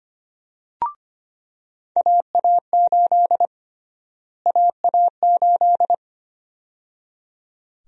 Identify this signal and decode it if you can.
Morse code, heard as a keyed tone, about 25 wpm, 700 Hz. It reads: AA8 AA8
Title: USA Call Sign Prefixes - Repeated 2x - 25wpm